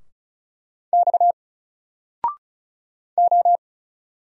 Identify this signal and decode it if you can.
Morse code, heard as a keyed tone, about 34 wpm, 700 Hz. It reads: X O
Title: Single Letter-Number Alt 35wpm